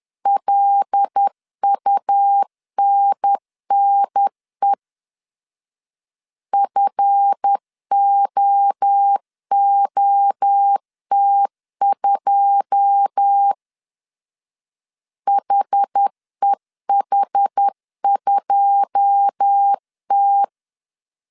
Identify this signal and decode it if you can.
Morse code, heard as a keyed tone, about 10 wpm, 800 Hz. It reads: LUNNE FOOT2 HEH2T
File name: CW / DET / 1 9 1500C S A 10wpm 4511